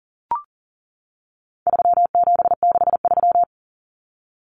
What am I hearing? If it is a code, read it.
Morse code, heard as a keyed tone, about 40 wpm, 700 Hz. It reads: 3763
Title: Common Cut Numbers - 4-Digit 40wpm